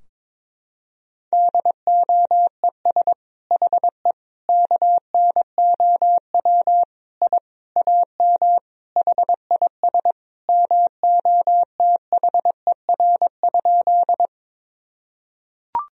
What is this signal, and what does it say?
Morse code, heard as a keyed tone, about 22 wpm, 700 Hz. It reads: DOES HE KNOW I AM HIS MOTHER?
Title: Sentences from Top 1000 Words - Review Part 2 - 22wpm